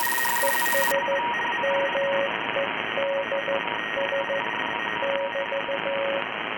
Morse code, heard as a keyed tone, about 15 wpm, 550 Hz, with noise. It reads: ES MEDS <BT>